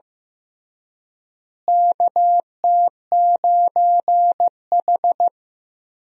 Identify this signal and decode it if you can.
Morse code, heard as a keyed tone, about 15 wpm, 700 Hz. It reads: KT9H
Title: USA Call Signs - 4 Characters - Repeated 3x - 15wpm